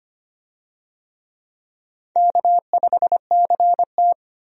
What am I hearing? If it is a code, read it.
Morse code, heard as a keyed tone, about 25 wpm, 700 Hz. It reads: K5CT
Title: USA Call Signs - 4 Characters - 25wpm